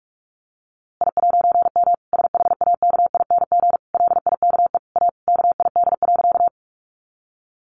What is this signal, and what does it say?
Morse code, heard as a keyed tone, about 44 wpm, 700 Hz. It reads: I'M SHAKING LIKE A KID.
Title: A Sound Of Thunder 45wpm